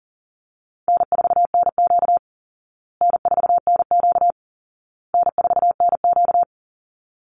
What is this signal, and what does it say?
Morse code, heard as a keyed tone, about 40 wpm, 700 Hz. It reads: N4NQ N4NQ N4NQ
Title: Call Signs Repeated 2x - 40wpm